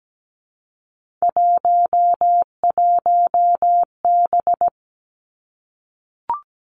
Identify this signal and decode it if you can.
Morse code, heard as a keyed tone, about 17 wpm, 700 Hz. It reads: 11B
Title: Three Challenging Letter-Number Combinations 17wpm